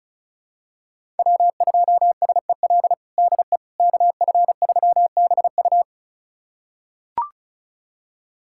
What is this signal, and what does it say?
Morse code, heard as a keyed tone, about 36 wpm, 700 Hz. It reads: W2SEL DE KF3BU
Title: USA Callsign DE Callsign - 5 Characters - 35wpm